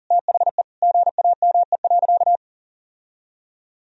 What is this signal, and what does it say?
Morse code, heard as a keyed tone, about 40 wpm, 700 Hz. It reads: THE GAME.
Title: Sentences from Top 1000 Words - Review Part 4 - 40wpm